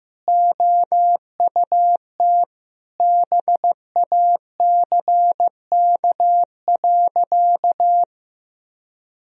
Morse code, heard as a keyed tone, about 15 wpm, 700 Hz. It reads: OUT BACK.
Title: Sentences from Top 1000 Words - Review Part 2 - 15wpm